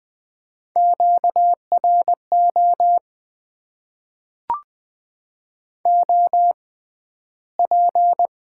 Morse code, heard as a keyed tone, about 20 wpm, spaced as for 12 wpm, 700 Hz. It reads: QRO OP
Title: Top 90 QSO Elements - 6x Character Spacing - 20wpm